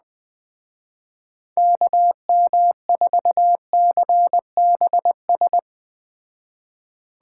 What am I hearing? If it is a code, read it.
Morse code, heard as a keyed tone, about 20 wpm, 700 Hz. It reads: KM4CBS